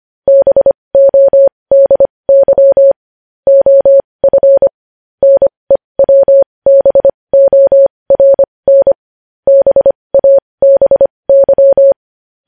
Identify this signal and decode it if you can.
Morse code, heard as a keyed tone, about 25 wpm, spaced as for 20 wpm, 550 Hz. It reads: BODY OF NEWBORN BABY
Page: News Headlines in Morse Code at 20 WPM